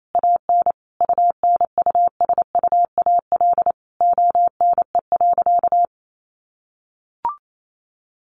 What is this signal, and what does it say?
Morse code, heard as a keyed tone, about 28 wpm, 700 Hz. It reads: AN UNUSUAL ONE.
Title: Sentences from Top 1400 Words 28wpm